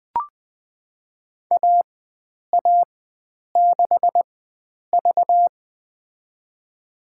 Morse code, heard as a keyed tone, about 21 wpm, spaced as for 9 wpm, 700 Hz. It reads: AA6V
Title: USA Call Signs - 4 Characters - 4x Character Spacing - 20wpm